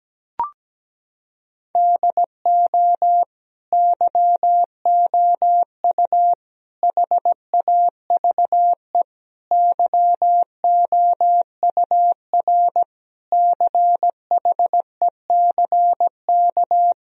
Morse code, heard as a keyed tone, about 17 wpm, 700 Hz. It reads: DO YOU HAVE YOUR CHECK
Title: Sentences from Top 1000 Words - Review Part 3 - 17wpm